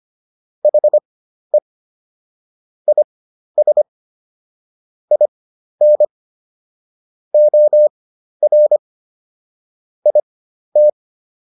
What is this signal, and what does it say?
Morse code, heard as a keyed tone, about 25 wpm, spaced as for 12 wpm, 600 Hz. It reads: HE IS IN OR IT